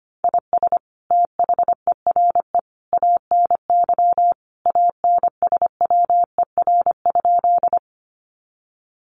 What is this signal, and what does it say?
Morse code, heard as a keyed tone, about 25 wpm, 700 Hz. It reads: IS THERE ANY ANSWER?
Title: Sentences from Top 300 Words 25wpm